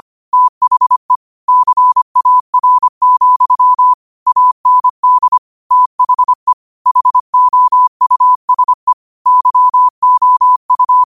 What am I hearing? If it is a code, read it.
Morse code, heard as a keyed tone, about 25 wpm, 1000 Hz. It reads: THE CAR, AND THE HOUSE YOU